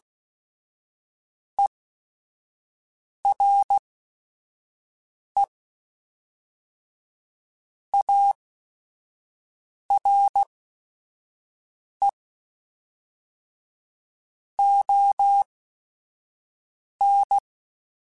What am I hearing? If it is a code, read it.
Morse code, heard as a keyed tone, about 16 wpm, spaced as for 5 wpm, 800 Hz. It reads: ERE ARE ON
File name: Technical Subjects / Technical Subjects-mp3s / AND Gate @5